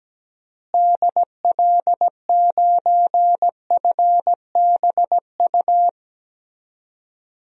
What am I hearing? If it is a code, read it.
Morse code, heard as a keyed tone, about 17 wpm, 700 Hz. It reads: DL9FBU
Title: DXCC-100 Call Signs - 6 Characters - Repeated 3x - 17wpm